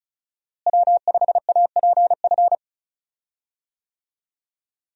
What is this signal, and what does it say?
Morse code, heard as a keyed tone, about 36 wpm, 700 Hz. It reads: W5APF
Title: USA Call Signs - 5 Characters - Repeated 3x - 35wpm